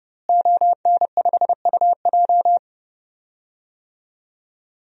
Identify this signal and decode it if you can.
Morse code, heard as a keyed tone, about 30 wpm, 700 Hz. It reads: ON5UJ
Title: DXCC-100 Call Signs - 5 Characters - Repeated 2x - 30wpm